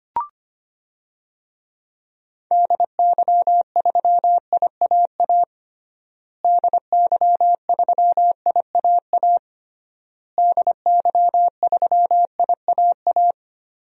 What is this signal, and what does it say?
Morse code, heard as a keyed tone, about 25 wpm, 700 Hz. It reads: DY3IAA DY3IAA DY3IAA
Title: DXCC-100 Call Signs - 6 Characters - Repeated 3x - 25wpm